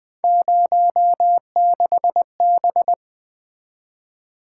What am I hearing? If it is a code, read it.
Morse code, heard as a keyed tone, about 23 wpm, 700 Hz. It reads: 06B
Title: Speed-Racing - Three Letter-Number Combinations - 20wpm